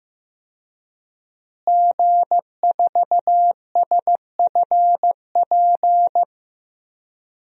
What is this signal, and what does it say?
Morse code, heard as a keyed tone, about 15 wpm, 700 Hz. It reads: G4SFP
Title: DXCC-100 Call Signs - 5 Characters - Repeated 2x - 15wpm